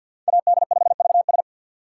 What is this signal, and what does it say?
Morse code, heard as a keyed tone, about 47 wpm, 700 Hz. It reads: AD5VS
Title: USA Call Signs - 5 Characters - 2x Character Spacing - 50wpm